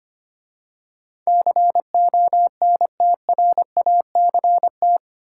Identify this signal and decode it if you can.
Morse code, heard as a keyed tone, about 25 wpm, 700 Hz. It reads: CONTRACT
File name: 8 Letter Words - First 3 Letters Given - 25wpm